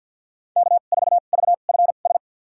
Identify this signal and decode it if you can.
Morse code, heard as a keyed tone, about 53 wpm, spaced as for 37 wpm, 700 Hz. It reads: K4VFS
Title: USA Call Signs - 5 Characters - 2x Character Spacing - 50wpm